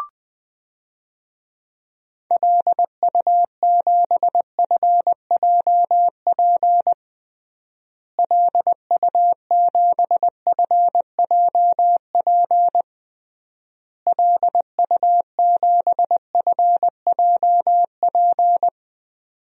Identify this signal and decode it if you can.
Morse code, heard as a keyed tone, about 20 wpm, 700 Hz. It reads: LU7FJP LU7FJP LU7FJP
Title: DXCC-100 Call Signs - 6 Characters - Repeated 3x - 20wpm